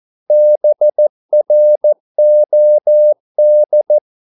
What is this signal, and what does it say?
Morse code, heard as a keyed tone, about 14 wpm, 600 Hz. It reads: BROD
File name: Korte ord DK 14wpm